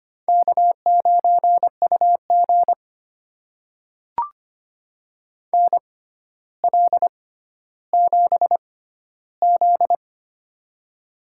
Morse code, heard as a keyed tone, about 25 wpm, spaced as for 13 wpm, 700 Hz. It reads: K9UG NL7Z
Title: USA Call Signs - 4 Characters - 6x Character Spacing - 25wpm